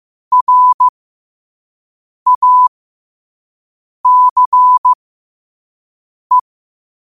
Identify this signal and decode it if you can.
Morse code, heard as a keyed tone, about 15 wpm, spaced as for 5 wpm, 1000 Hz. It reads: RACE
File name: qotdfiles / 5wpm